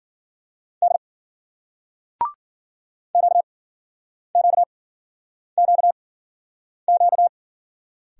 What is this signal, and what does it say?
Morse code, heard as a keyed tone, about 50 wpm, spaced as for 9 wpm, 700 Hz. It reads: D QQQQ
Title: Speed-Racing Single Letters 40wpm